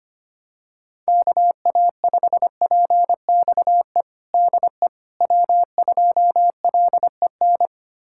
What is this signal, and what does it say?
Morse code, heard as a keyed tone, about 25 wpm, 700 Hz. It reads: KA5PXE DE W2LEN
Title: USA Callsign DE Callsign 25wpm